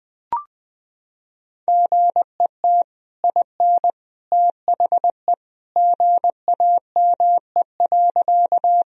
Morse code, heard as a keyed tone, about 21 wpm, 700 Hz. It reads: GET IN THE GAME.
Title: Sentences from Top 1000 Words - Review Part 3 - 20wpm